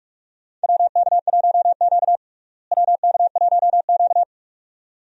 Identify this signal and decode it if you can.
Morse code, heard as a keyed tone, about 42 wpm, 700 Hz. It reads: WK1Q WK1Q
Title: USA Call Signs - 4 Characters - Repeated 2x - 45wpm